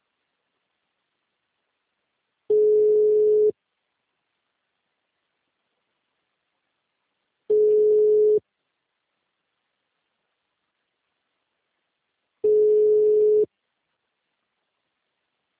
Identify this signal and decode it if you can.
Ringback tone (Europe, 425 Hz).